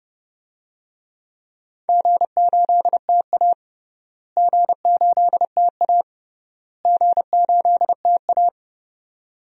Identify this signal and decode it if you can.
Morse code, heard as a keyed tone, about 30 wpm, 700 Hz. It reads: G8TA G8TA G8TA
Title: DXCC-100 Call Signs - 4 Characters - Repeated 3x - 30wpm